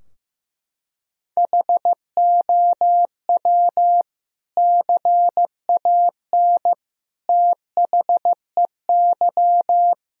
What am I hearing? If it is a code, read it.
Morse code, heard as a keyed tone, about 15 wpm, 700 Hz. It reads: HOW CAN THEY
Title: Sentences from Top 1000 Words - Review Part 3 - 15wpm